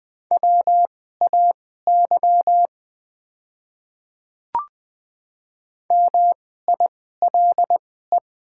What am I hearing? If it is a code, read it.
Morse code, heard as a keyed tone, about 20 wpm, spaced as for 14 wpm, 700 Hz. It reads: WAY MILE
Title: 2x Character Spacing - Top 300 Words - 20wpm